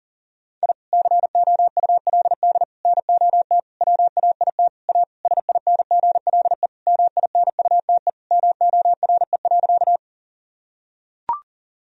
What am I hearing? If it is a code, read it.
Morse code, heard as a keyed tone, about 37 wpm, 700 Hz. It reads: I COULD NOT WAIT A SINGLE MINUTE MORE.
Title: Sentences from Top 1000 Words - Review Part 4 - 40wpm